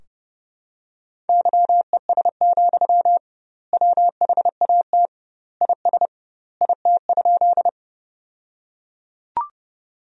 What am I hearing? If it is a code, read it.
Morse code, heard as a keyed tone, about 32 wpm, 700 Hz. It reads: YES, WHAT IS IT?
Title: 2x Word-Spacing - Sentences from Top 500 Words - 30wpm